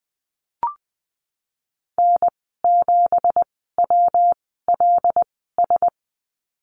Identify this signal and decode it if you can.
Morse code, heard as a keyed tone, about 19 wpm, spaced as for 14 wpm, 700 Hz. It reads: N7WLS
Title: 179 Lesson 29g - Slash - Call Signs 20wpm